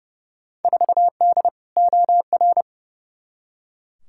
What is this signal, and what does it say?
Morse code, heard as a keyed tone, about 30 wpm, 700 Hz. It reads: 4D OR
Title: Field Day Exchanges Encore - 30wpm